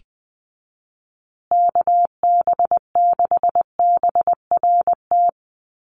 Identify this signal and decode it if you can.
Morse code, heard as a keyed tone, about 20 wpm, 700 Hz. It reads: KB6BRT
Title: USA Call Signs Repeated 3x 20wpm